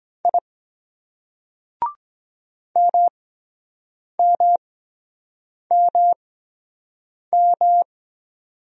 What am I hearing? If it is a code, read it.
Morse code, heard as a keyed tone, about 23 wpm, 700 Hz. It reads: I M M M M